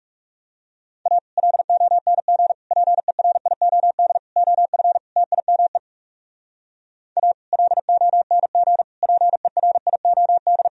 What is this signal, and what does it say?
Morse code, heard as a keyed tone, about 37 wpm, 700 Hz. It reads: A LONG PERIOD OF TIME A LONG PERIOD
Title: Speed-Racing - Sets of 5 Words 30wpm